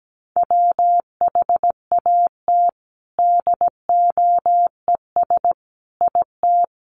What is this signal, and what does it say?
Morse code, heard as a keyed tone, about 17 wpm, 700 Hz. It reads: WHAT DOES IT
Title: Sentences from Top 1300 Words 17wpm